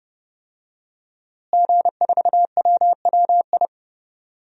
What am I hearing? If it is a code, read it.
Morse code, heard as a keyed tone, about 30 wpm, 700 Hz. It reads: G4WWI
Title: DXCC-100 Call Signs - 5 Characters - Repeated 3x - 30wpm